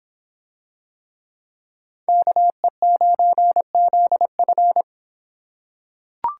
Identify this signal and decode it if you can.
Morse code, heard as a keyed tone, about 25 wpm, 700 Hz. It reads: KE9ZF